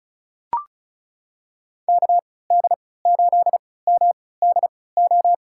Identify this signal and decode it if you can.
Morse code, heard as a keyed tone, about 34 wpm, spaced as for 20 wpm, 700 Hz. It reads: KD8MDO
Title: USA Call Signs - 6 Characters - 3x Character Spacing - 35wpm